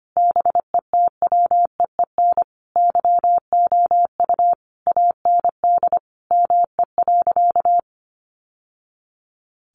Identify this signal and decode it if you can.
Morse code, heard as a keyed tone, about 25 wpm, 700 Hz. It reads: BETWEEN YOU AND ME.
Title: Sentences from Top 1200 Words 25wpm